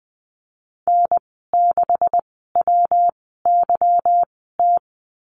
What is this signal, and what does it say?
Morse code, heard as a keyed tone, about 20 wpm, spaced as for 15 wpm, 700 Hz. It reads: N6WYT
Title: 147 Lesson 24g - Letter y - Call Signs 20wpm